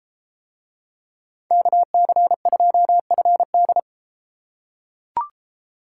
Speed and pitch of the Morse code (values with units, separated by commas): 32 wpm, 700 Hz